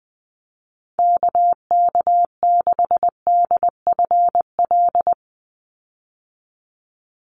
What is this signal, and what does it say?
Morse code, heard as a keyed tone, about 20 wpm, 700 Hz. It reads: KK6DFL
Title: USA Call Signs 20wpm